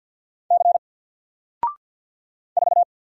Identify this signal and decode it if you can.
Morse code, heard as a keyed tone, about 50 wpm, 700 Hz. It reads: C 4
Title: Single Character (Letter:Number:Punctuation:Prosign) Alt 50wpm